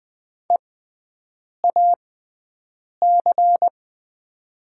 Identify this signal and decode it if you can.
Morse code, heard as a keyed tone, about 20 wpm, spaced as for 7 wpm, 700 Hz. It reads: EAC
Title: 6x Character Spacing - Top 100 Words - 20wpm